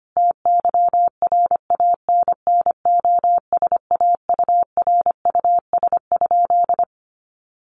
Morse code, heard as a keyed tone, about 25 wpm, 700 Hz. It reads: TYRANNOSAURUS?
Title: A Sound Of Thunder 25wpm